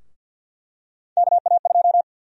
Morse code, heard as a keyed tone, about 50 wpm, 700 Hz. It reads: KA2
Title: USA Call Sign Prefixes - Repeated 2x - 50wpm